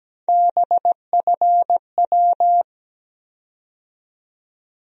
Morse code, heard as a keyed tone, about 17 wpm, 700 Hz. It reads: BFW
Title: USA Call Sign Postfixes - 3 Letters 17wpm